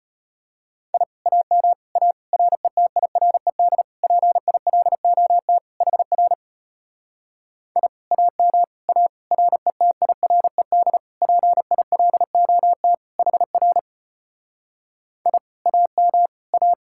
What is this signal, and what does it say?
Morse code, heard as a keyed tone, about 33 wpm, 700 Hz. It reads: I AM A RETIRED PILOT HR I AM A RETIRED PILOT HR I AM A